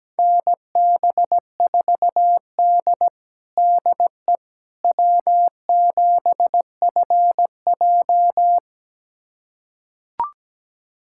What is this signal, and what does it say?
Morse code, heard as a keyed tone, about 17 wpm, 700 Hz. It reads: NB4D DE W7FJ
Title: USA Callsign DE Callsign - 4 Characters - 17wpm